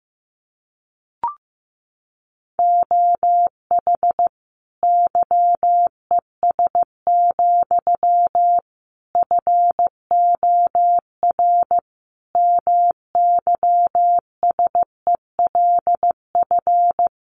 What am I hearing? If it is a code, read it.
Morse code, heard as a keyed tone, about 15 wpm, 700 Hz. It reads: OH YES, FOR MYSELF
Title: Sentences from Top 1300 Words 15wpm